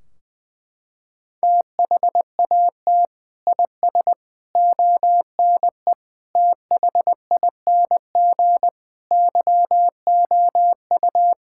Morse code, heard as a keyed tone, about 21 wpm, 700 Hz. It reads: THAT IS ONE THING YOU
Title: Sentences from Top 1600 Words 20wpm